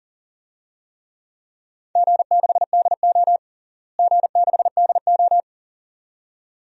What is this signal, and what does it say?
Morse code, heard as a keyed tone, about 40 wpm, 700 Hz. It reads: G6DO G6DO
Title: DXCC-100 Call Signs - 4 Characters - Repeated 2x - 40wpm